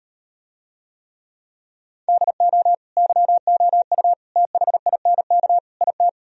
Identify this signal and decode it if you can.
Morse code, heard as a keyed tone, about 38 wpm, 700 Hz. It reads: DO YOU THINK IT